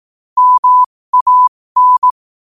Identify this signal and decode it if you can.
Morse code, heard as a keyed tone, about 18 wpm, spaced as for 15 wpm, 1000 Hz. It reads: MAN
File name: qotdfiles / 15wpm